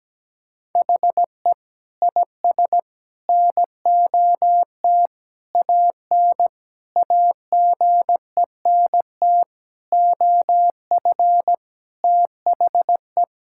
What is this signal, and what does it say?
Morse code, heard as a keyed tone, about 17 wpm, 700 Hz. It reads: HE IS NOT AN AGENT OF THE
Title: Sentences from Top 1600 Words 17wpm